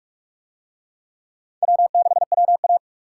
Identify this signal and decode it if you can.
Morse code, heard as a keyed tone, about 41 wpm, 700 Hz. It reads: W6WA